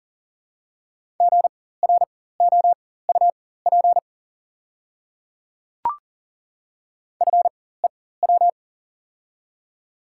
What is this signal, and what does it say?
Morse code, heard as a keyed tone, about 40 wpm, spaced as for 19 wpm, 690 Hz. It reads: GROUP FEW
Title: 4x Character Spacing - Top 500 Words - 40wpm